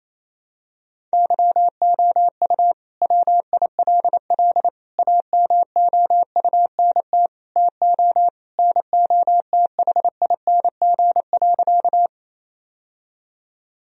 Morse code, heard as a keyed tone, about 28 wpm, 700 Hz. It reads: YOU WILL AMOUNT TO NOTHING.
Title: Sentences from Top 1000 Words - Review Part 4 - 28wpm